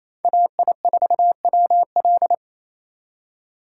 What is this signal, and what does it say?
Morse code, heard as a keyed tone, about 35 wpm, 700 Hz. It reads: AI4WL